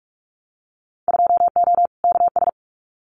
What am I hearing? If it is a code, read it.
Morse code, heard as a keyed tone, about 44 wpm, 700 Hz. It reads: 2O KS
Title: Winter Field Day Exchanges Encore 45wpm